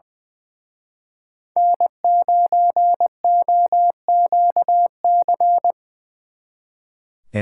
Morse code, heard as a keyed tone, about 24 wpm, 700 Hz. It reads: N9OQC
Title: Speed Racing - USA Call Signs 20wpm